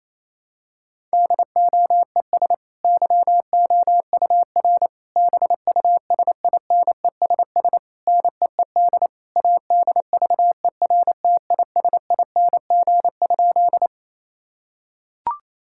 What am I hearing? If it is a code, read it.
Morse code, heard as a keyed tone, about 28 wpm, 700 Hz. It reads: DOES YOUR BUSINESS NEED ADVERTISING?
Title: Sentences from Top 1000 Words - Review Part 6 - 28wpm